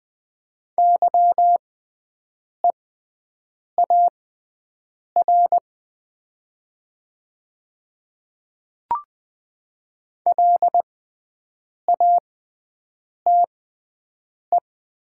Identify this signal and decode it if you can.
Morse code, heard as a keyed tone, about 20 wpm, spaced as for 7 wpm, 700 Hz. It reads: YEAR LATE
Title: 6x Character Spacing - Top 500 Words - 20wpm